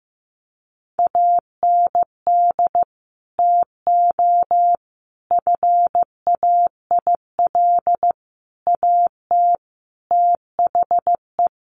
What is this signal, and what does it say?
Morse code, heard as a keyed tone, about 15 wpm, 700 Hz. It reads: AND TO FAIL AT THE
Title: Sentences from Top 1100 Words 15wpm